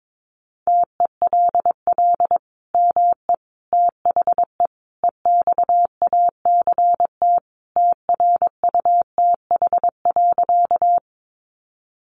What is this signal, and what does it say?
Morse code, heard as a keyed tone, about 22 wpm, 700 Hz. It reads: TELL ME THE EXACT TRUTH.
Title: Sentences from Top 1300 Words 22wpm